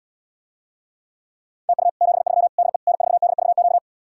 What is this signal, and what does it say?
Morse code, heard as a keyed tone, about 80 wpm, 700 Hz. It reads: TU 73 DE N4DVQ